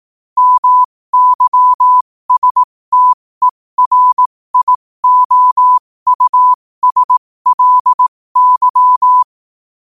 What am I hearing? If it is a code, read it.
Morse code, heard as a keyed tone, about 18 wpm, spaced as for 15 wpm, 1000 Hz. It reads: MYSTERIOUSLY
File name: qotdfiles / 15wpm